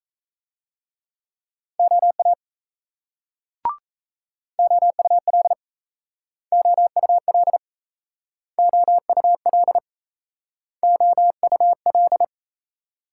Morse code, heard as a keyed tone, about 37 wpm, 700 Hz. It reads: OA OUL OUL OUL OUL